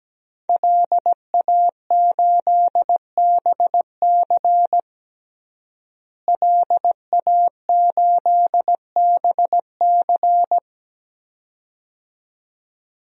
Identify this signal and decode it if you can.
Morse code, heard as a keyed tone, about 17 wpm, 700 Hz. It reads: LA8BC LA8BC
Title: DXCC-100 Call Signs - 5 Characters - Repeated 2x - 17wpm